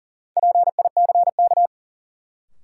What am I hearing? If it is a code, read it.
Morse code, heard as a keyed tone, about 40 wpm, 700 Hz. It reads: PICK
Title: Sentences from Top 1000 Words - New Word Drill - 40wpm